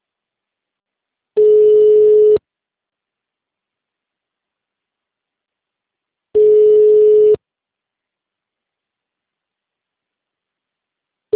Ringback tone (Europe, 425 Hz).